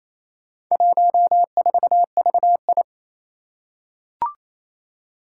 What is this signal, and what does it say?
Morse code, heard as a keyed tone, about 28 wpm, 700 Hz. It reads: 14VI
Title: Four Challenging Letter-Number Combinations 28wpm